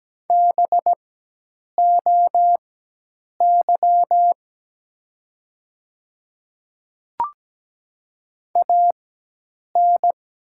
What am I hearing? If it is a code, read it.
Morse code, heard as a keyed tone, about 17 wpm, spaced as for 8 wpm, 700 Hz. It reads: BOY AN
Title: 4x Character Spacing - Top 200 Words - 17wpm